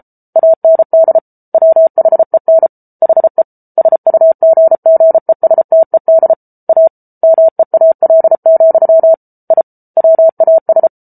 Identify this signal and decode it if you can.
Morse code, heard as a keyed tone, about 34 wpm, 650 Hz. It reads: AND WHEN HE SUGGESTED A MEAL, I WAS